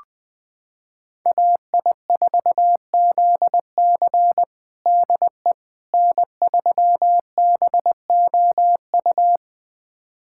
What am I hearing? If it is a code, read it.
Morse code, heard as a keyed tone, about 20 wpm, 700 Hz. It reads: AI4ZC DE N3BOU
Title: USA Callsign DE Callsign - 5 Characters - 20wpm